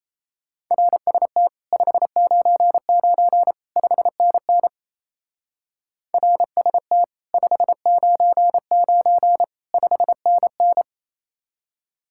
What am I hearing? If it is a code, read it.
Morse code, heard as a keyed tone, about 36 wpm, 700 Hz. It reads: RST 599 5NN RST 599 5NN